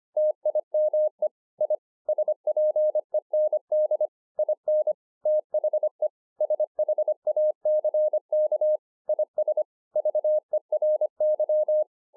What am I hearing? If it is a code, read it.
Morse code, heard as a keyed tone, about 25 wpm, 610 Hz. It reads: TIME I SPEND IN THE SHACK IS VERY
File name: 2011-04-mp3 / THIS IS NOT MY SHACK 2 25WPM